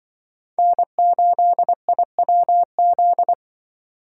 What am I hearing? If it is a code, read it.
Morse code, heard as a keyed tone, about 25 wpm, 700 Hz. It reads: N8IWZ
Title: Speed Racing - USA 1x3 Call Signs 20wpm